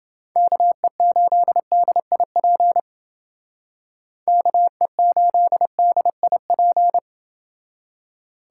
Morse code, heard as a keyed tone, about 28 wpm, 700 Hz. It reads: KE8DIP KE8DIP